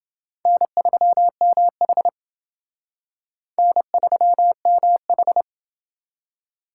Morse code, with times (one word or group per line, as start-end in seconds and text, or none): 0.5-2.1 N3MH
3.6-5.4 N3MH